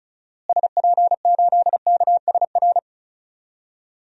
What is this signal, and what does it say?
Morse code, heard as a keyed tone, about 36 wpm, 700 Hz. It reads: SP8KSR
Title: DXCC-100 Call Signs - 6 Characters - Repeated 3x - 35wpm